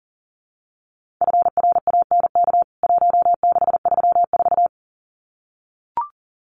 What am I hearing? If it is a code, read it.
Morse code, heard as a keyed tone, about 40 wpm, 700 Hz. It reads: FRANK 1634
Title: CWT Exchange 40wpm